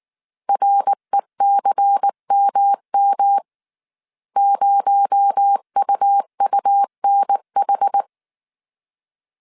Morse code, heard as a keyed tone, about 19 wpm, 800 Hz. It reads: RECMM 0UUNH
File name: CW / DET / 1 6 1500C S C 18wpm 2356